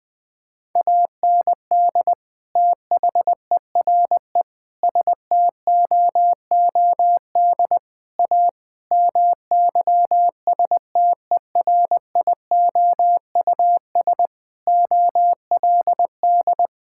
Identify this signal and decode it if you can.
Morse code, heard as a keyed tone, about 20 wpm, 700 Hz. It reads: AND THERE STOOD A MYSTERIOUS OLD
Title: Sentences with 2-Word Phrases - Spoken Beginning - 20wpm